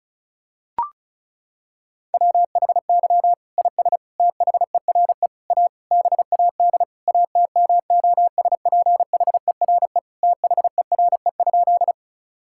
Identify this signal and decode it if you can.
Morse code, heard as a keyed tone, about 36 wpm, 700 Hz. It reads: WHY IS THERE A BAD ATMOSPHERE THERE?
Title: Sentences from Top 1600 Words 35wpm